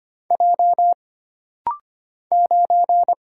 Morse code, heard as a keyed tone, about 25 wpm, 700 Hz. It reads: J 9
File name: Single Character (Letter:Number:Punctuation:Prosign) - Rapid-Fire - 25wpm